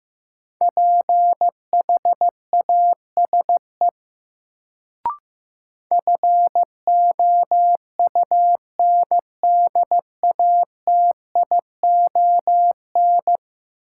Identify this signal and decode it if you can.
Morse code, heard as a keyed tone, about 15 wpm, 700 Hz. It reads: PHASE FOUNDATION
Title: Sentences from Top 1200 Words - New Word Drill - 15wpm